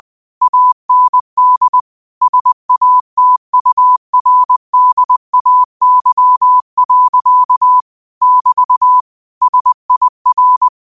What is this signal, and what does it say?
Morse code, heard as a keyed tone, about 20 wpm, 1000 Hz. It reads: AND SATURDAY. <BT> SIR
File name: qotdfiles / 20wpm